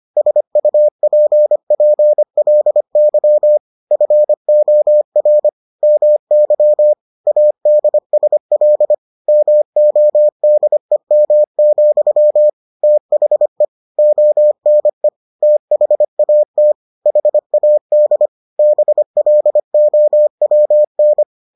Morse code, heard as a keyed tone, about 25 wpm, 600 Hz. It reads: SUPPLY FOR MY ADSL MODEM, THE ONE THAT HAD BLOWN